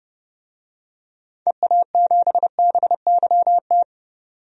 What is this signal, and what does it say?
Morse code, heard as a keyed tone, about 30 wpm, 700 Hz. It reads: EA7BYT
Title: DXCC-100 Call Signs - 6 Characters - 30wpm